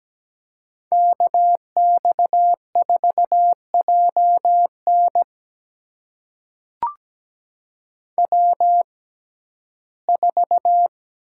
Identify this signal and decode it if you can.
Morse code, heard as a keyed tone, about 17 wpm, spaced as for 12 wpm, 700 Hz. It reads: KX4JN W4
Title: USA Call Signs - 5 Characters - 6x Character Spacing - 17wpm